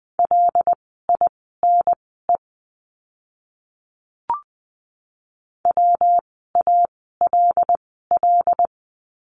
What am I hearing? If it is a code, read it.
Morse code, heard as a keyed tone, about 20 wpm, spaced as for 14 wpm, 700 Hz. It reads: LINE WALL
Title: 115 Lesson 19f - Letter w - Words 20wpm